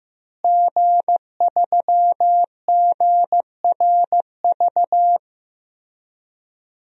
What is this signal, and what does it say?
Morse code, heard as a keyed tone, about 15 wpm, 700 Hz. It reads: G3GRV
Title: DXCC-100 Call Signs - 5 Characters - Repeated 3x - 15wpm